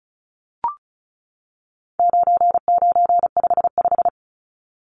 Morse code, heard as a keyed tone, about 34 wpm, 700 Hz. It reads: 9955
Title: Numbers - 4-Digit 35wpm